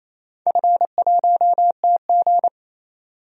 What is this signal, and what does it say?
Morse code, heard as a keyed tone, about 28 wpm, 700 Hz. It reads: F1TG
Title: DXCC-100 Call Signs - 4 Characters - 28wpm